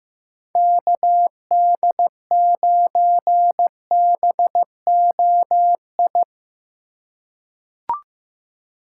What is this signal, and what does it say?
Morse code, heard as a keyed tone, about 15 wpm, 700 Hz. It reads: KD9BOI
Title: USA Call Signs - 3x Character Spacing - 15wpm